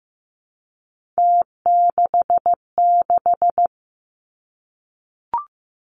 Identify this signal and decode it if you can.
Morse code, heard as a keyed tone, about 15 wpm, 700 Hz. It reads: T66
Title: Three Challenging Letter-Number Combinations 15wpm